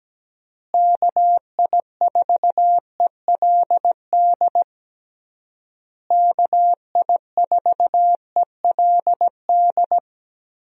Morse code, heard as a keyed tone, about 17 wpm, 700 Hz. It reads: KI4ELD KI4ELD
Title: USA Call Signs - 6 Characters - Repeated 3x - 17wpm